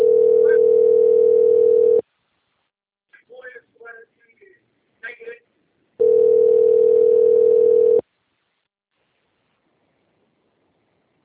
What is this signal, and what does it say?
Ringback tone (North America), 440 + 480 Hz, with speech.
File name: out- - - - - .6660